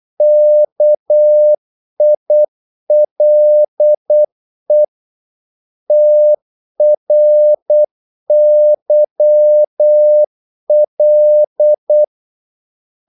Morse code, heard as a keyed tone, about 8 wpm, 600 Hz. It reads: KILE TRYL
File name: Korte ord DK 8wpm